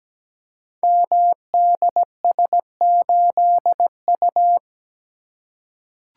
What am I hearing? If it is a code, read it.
Morse code, heard as a keyed tone, about 17 wpm, 700 Hz. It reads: MDS8U
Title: Code Groups - 5 Letter:Numbers - 17wpm